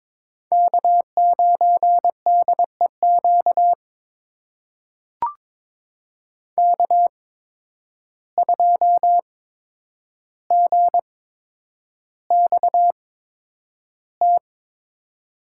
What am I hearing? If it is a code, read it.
Morse code, heard as a keyed tone, about 23 wpm, spaced as for 10 wpm, 700 Hz. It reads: K9DEQ K2GXT
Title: USA Call Signs - 5 Characters - 8x Character Spacing - 22wpm